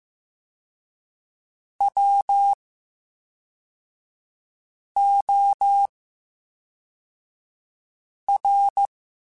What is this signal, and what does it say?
Morse code, heard as a keyed tone, about 15 wpm, spaced as for 3 wpm, 800 Hz. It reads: WOR
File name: Special INGs / Special INGs-mp3s / VerySlowToFaster Ing @10